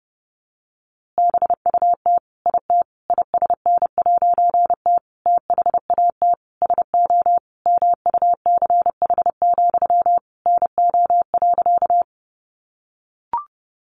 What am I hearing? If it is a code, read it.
Morse code, heard as a keyed tone, about 30 wpm, 700 Hz. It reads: BUT IT ISN'T THAT SO MUCH, NO.
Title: A Sound Of Thunder 30wpm